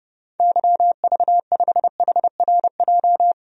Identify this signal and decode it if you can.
Morse code, heard as a keyed tone, about 32 wpm, 700 Hz. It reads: YV5HRJ
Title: DXCC-100 Call Signs - 6 Characters - 30wpm